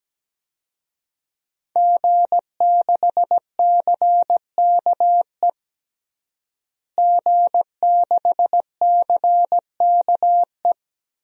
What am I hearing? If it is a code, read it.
Morse code, heard as a keyed tone, about 17 wpm, 700 Hz. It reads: G6CKE G6CKE
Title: DXCC-100 Call Signs - 5 Characters - Repeated 2x - 17wpm